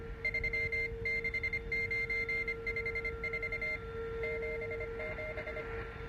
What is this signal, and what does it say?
Morse code, heard as a keyed tone, about 25 wpm, 2100 Hz, with noise. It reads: 36954 77